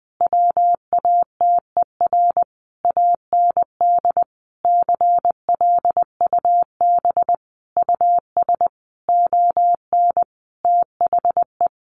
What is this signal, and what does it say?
Morse code, heard as a keyed tone, about 20 wpm, 700 Hz. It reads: WATER AND CLUB US ON THE
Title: Sentences from Top 1200 Words 20wpm